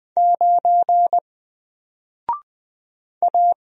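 Morse code, heard as a keyed tone, about 20 wpm, 700 Hz. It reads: 9 A
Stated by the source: Single letter and number Morse drill at 20 wpm, from N4LCD.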